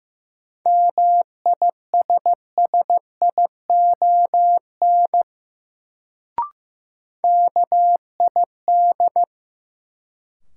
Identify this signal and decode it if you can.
Morse code, heard as a keyed tone, about 15 wpm, 700 Hz. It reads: MISSION KID
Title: Sentences from Top 1600 Words - New Word Drill Alt - 15wpm